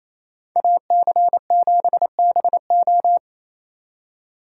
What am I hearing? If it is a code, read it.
Morse code, heard as a keyed tone, about 34 wpm, 700 Hz. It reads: AC7BO
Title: Speed Racing - USA 2x2 Call Signs 28wpm